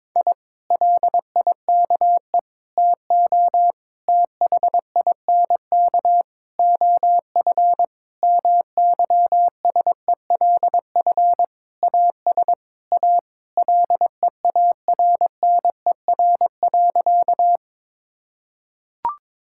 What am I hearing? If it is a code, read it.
Morse code, heard as a keyed tone, about 22 wpm, 700 Hz. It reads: I LIKE TO THINK OF MYSELF AS A LEARNER.
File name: practice / 22wpm / Sentences with 8-Word Phrases - 22wpm